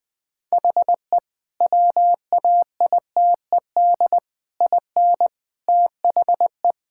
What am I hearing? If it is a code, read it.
Morse code, heard as a keyed tone, about 21 wpm, 700 Hz. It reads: HE WAITED IN THE
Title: Sentences with 4-Word Phrases - 20wpm